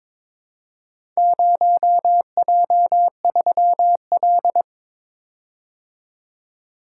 Morse code, heard as a keyed tone, about 23 wpm, 700 Hz. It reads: 0J3L
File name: practice / 22wpm / Four Letter-Number Combinations - 22wpm